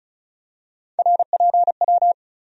Morse code, heard as a keyed tone, about 34 wpm, 700 Hz. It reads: RPW
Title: USA Call Sign Postfixes - 3 Letters 35wpm